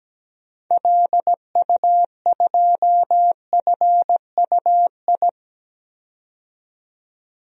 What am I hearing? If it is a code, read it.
Morse code, heard as a keyed tone, about 17 wpm, 700 Hz. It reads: LU2FUI
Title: DXCC-100 Call Signs - 6 Characters - 17wpm